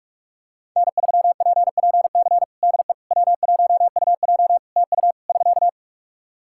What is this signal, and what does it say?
Morse code, heard as a keyed tone, about 46 wpm, 700 Hz. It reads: N3PPC DE W1UJ TU <SK>
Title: Practice QSO - Exchange 4 - Ending - 45wpm